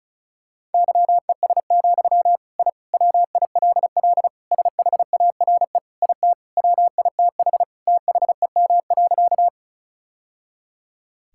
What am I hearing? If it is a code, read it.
Morse code, heard as a keyed tone, about 36 wpm, 690 Hz. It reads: YES, I WILL SHARE IT WITH THEM.
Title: Sentences from Top 1000 Words - Review Part 6 - 35wpm